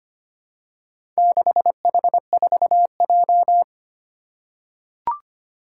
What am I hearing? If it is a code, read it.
Morse code, heard as a keyed tone, about 25 wpm, 700 Hz. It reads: 6H4J
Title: Four Challenging Letter-Number Combinations 25wpm